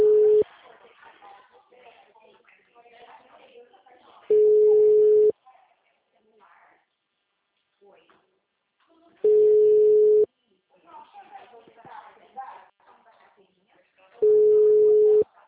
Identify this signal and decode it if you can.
Ringback tone (Europe, 425 Hz), with speech.